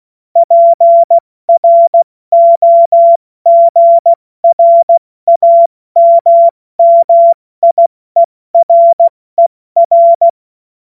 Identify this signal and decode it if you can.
Morse code, heard as a keyed tone, about 16 wpm, 680 Hz. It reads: PROGRAMMIERER
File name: Morsekurs / Witze / Witz 05. April 2022 70 F80